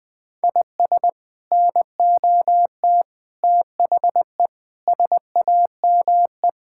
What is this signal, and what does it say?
Morse code, heard as a keyed tone, about 20 wpm, 700 Hz. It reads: IS NOT THE SAME
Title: Sets of 6 Words - 20wpm